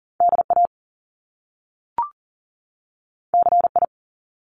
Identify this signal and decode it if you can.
Morse code, heard as a keyed tone, about 38 wpm, 700 Hz. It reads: DA CI
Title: Top Two Letter Combinations in Words 40wpm